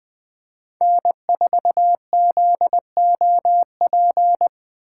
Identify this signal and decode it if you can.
Morse code, heard as a keyed tone, about 21 wpm, 700 Hz. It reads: N4ZOP
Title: USA Call Signs - 5 Characters - 3x Character Spacing - 20wpm